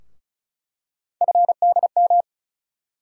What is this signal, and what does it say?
Morse code, heard as a keyed tone, about 36 wpm, 700 Hz. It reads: FDM
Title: USA Call Sign Postfixes - 3 Letters 35wpm